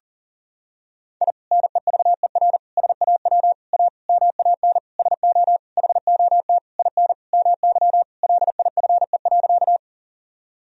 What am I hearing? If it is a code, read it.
Morse code, heard as a keyed tone, about 40 wpm, 700 Hz. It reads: I NEVER SAW A MAN SO HOT IN MY LIFE.
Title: Sentences from Top 1000 Words - Review Part 3 - 40wpm